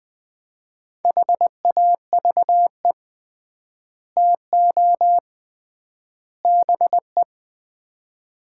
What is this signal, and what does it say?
Morse code, heard as a keyed tone, about 20 wpm, 700 Hz. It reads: HAVE TO BE
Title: 3x Word-Spacing - Sets of 3 Words - 20wpm